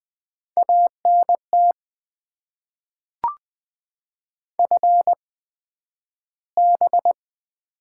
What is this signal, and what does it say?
Morse code, heard as a keyed tone, about 20 wpm, spaced as for 11 wpm, 700 Hz. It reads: ANT FB
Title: Top 20 QSO Elements - 8x Character Spacing - 20wpm